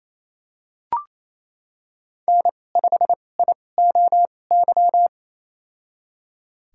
Morse code, heard as a keyed tone, about 28 wpm, spaced as for 20 wpm, 700 Hz. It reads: N5IOY
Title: USA Call Signs - 5 Characters - 2x Character Spacing - 28wpm